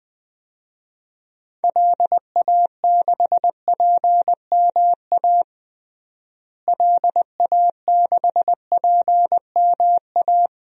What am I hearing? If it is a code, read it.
Morse code, heard as a keyed tone, about 20 wpm, 700 Hz. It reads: LA6PMA LA6PMA
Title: DXCC-100 Call Signs - 6 Characters - Repeated 2x - 20wpm